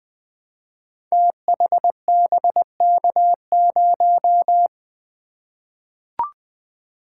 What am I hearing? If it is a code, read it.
Morse code, heard as a keyed tone, about 20 wpm, 700 Hz. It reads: THBK0
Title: Code Groups - 5 Letter:Numbers - 20wpm